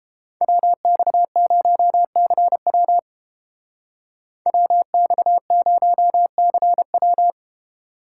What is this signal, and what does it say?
Morse code, heard as a keyed tone, about 29 wpm, 700 Hz. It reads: WX0CW WX0CW